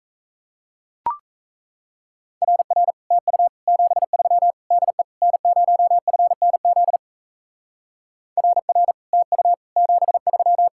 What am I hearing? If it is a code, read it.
Morse code, heard as a keyed tone, about 36 wpm, 700 Hz. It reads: RR TU 73 DE N0FNZ RR TU 73